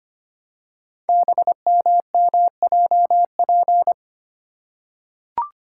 Morse code, heard as a keyed tone, about 25 wpm, 700 Hz. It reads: BMMJP
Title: Code Groups - 5 Letter:Numbers - 25wpm